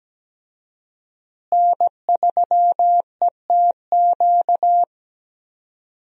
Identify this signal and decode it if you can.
Morse code, heard as a keyed tone, about 17 wpm, 700 Hz. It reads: N3ETQ
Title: USA Call Signs - 5 Characters - Repeated 2x - 17wpm